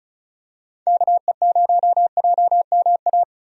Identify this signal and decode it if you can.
Morse code, heard as a keyed tone, about 36 wpm, 700 Hz. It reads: KE0JMA
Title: USA Call Signs - 6 Characters - 35wpm